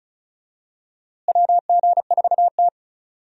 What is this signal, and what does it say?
Morse code, heard as a keyed tone, about 36 wpm, 700 Hz. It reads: WG4T
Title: USA Call Signs - 4 Characters - Repeated 2x - 35wpm